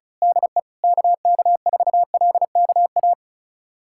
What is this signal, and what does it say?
Morse code, heard as a keyed tone, about 36 wpm, 700 Hz. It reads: DE KK4LKA
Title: USA Callsign DE Callsign 35wpm